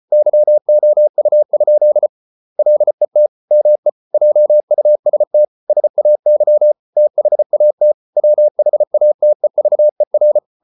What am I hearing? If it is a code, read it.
Morse code, heard as a keyed tone, about 34 wpm, 600 Hz. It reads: YOU? LET ME JUST SAY THAT WHATEVER